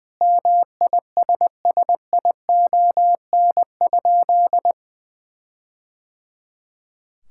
Morse code, heard as a keyed tone, about 20 wpm, 700 Hz. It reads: MISSION?
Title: Sentences from Top 1600 Words 20wpm